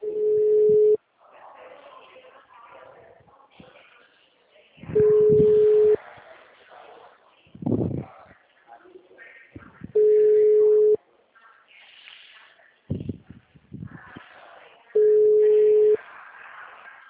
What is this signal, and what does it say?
Ringback tone (Europe, 425 Hz), with speech.